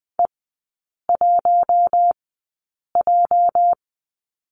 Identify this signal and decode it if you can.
Morse code, heard as a keyed tone, about 20 wpm, spaced as for 8 wpm, 690 Hz. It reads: E1J
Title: 212 Lesson 34e - Number 8 - Three Characters 20wpm